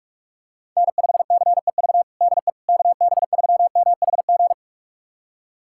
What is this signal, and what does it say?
Morse code, heard as a keyed tone, about 41 wpm, 700 Hz. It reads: N5CEV DE KB3MHG
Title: USA Callsign DE Callsign 45wpm